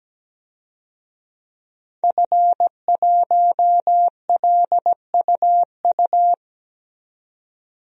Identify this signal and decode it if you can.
Morse code, heard as a keyed tone, about 17 wpm, 700 Hz. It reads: F1LUU
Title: DXCC-100 Call Signs - 5 Characters - 17wpm